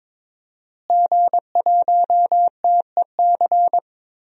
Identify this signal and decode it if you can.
Morse code, heard as a keyed tone, about 22 wpm, 700 Hz. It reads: G1TEC
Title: DXCC-100 Call Signs - 5 Characters - Repeated 3x - 22wpm